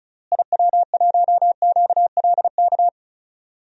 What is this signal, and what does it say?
Morse code, heard as a keyed tone, about 36 wpm, 700 Hz. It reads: IW1QLK
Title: DXCC-100 Call Signs - 6 Characters - Repeated 2x - 35wpm